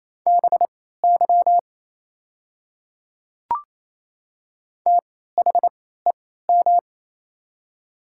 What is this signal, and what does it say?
Morse code, heard as a keyed tone, about 28 wpm, spaced as for 16 wpm, 700 Hz. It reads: BY THEM
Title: 3x Character Spacing - Top 100 Words - 28wpm